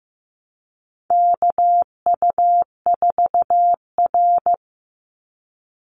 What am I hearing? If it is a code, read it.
Morse code, heard as a keyed tone, about 15 wpm, 700 Hz. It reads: KU4R
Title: NA SOTA Chaser Callsigns 15wpm